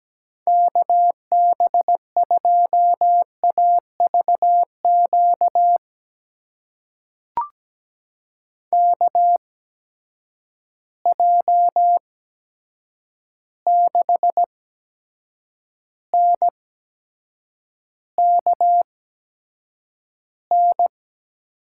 Morse code, heard as a keyed tone, about 17 wpm, spaced as for 7 wpm, 700 Hz. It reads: KB2AVQ KJ6NKN
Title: USA Call Signs - 6 Characters - 8x Character Spacing - 17wpm